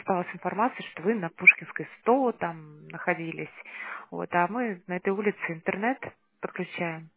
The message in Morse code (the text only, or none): none